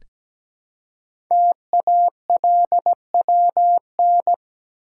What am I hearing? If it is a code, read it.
Morse code, heard as a keyed tone, about 17 wpm, 700 Hz. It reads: TALWN
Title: Code Groups - 5 Letters - 17wpm